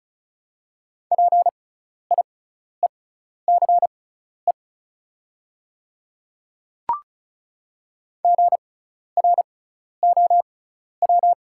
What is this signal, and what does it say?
Morse code, heard as a keyed tone, about 34 wpm, spaced as for 12 wpm, 690 Hz. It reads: PIECE GROW
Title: 6x Character Spacing - Top 500 Words - 35wpm